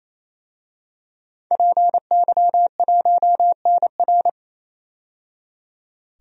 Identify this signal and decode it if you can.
Morse code, heard as a keyed tone, about 27 wpm, 700 Hz. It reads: PY1NR
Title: DXCC-100 Call Signs - 5 Characters - 28wpm